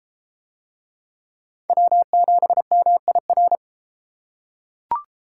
Morse code, heard as a keyed tone, about 30 wpm, 700 Hz. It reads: W7MIR